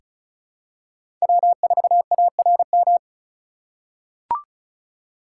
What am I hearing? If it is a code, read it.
Morse code, heard as a keyed tone, about 36 wpm, 700 Hz. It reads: W4ARM
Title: USA Call Signs - 5 Characters - 4x Character Spacing - 35wpm